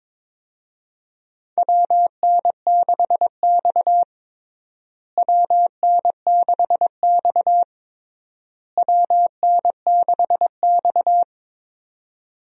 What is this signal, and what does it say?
Morse code, heard as a keyed tone, about 22 wpm, 700 Hz. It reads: WN6X WN6X WN6X
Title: USA Call Signs - 4 Characters - Repeated 3x - 22wpm